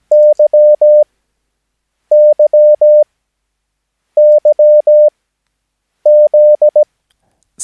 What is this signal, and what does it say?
Morse code, heard as a keyed tone, about 17 wpm, spaced as for 7 wpm, 600 Hz. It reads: YYYZ
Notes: Övningsmaterial med alfabetet A till Ö samt siffror 0 till 9 enligt International/Continental Morse Code.